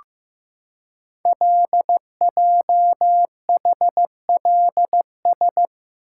Morse code, heard as a keyed tone, about 15 wpm, 700 Hz. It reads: LJHLS
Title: Code Groups - 5 Letters - 15wpm